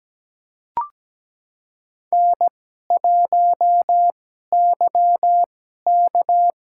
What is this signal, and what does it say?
Morse code, heard as a keyed tone, about 17 wpm, spaced as for 12 wpm, 700 Hz. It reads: N1YK
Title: USA Call Signs - 4 Characters - 2x Character Spacing - 17wpm